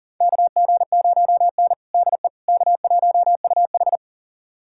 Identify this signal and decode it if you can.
Morse code, heard as a keyed tone, about 40 wpm, 700 Hz. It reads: KG0N DE K1UH
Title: USA Callsign DE Callsign - 4 Characters - 40wpm